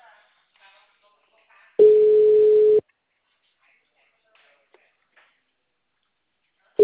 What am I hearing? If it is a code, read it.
Ringback tone (Europe, 425 Hz).
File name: out- -2024- - - .1061